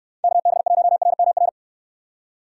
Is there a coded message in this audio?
Yes: Morse code from 0.2 to 1.5 s, heard as a keyed tone, about 66 wpm, 700 Hz. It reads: KB1RMR